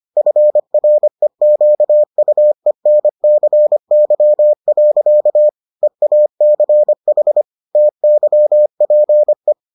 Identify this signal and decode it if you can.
Morse code, heard as a keyed tone, about 25 wpm, 600 Hz. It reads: FREQUENCY. EACH TYPE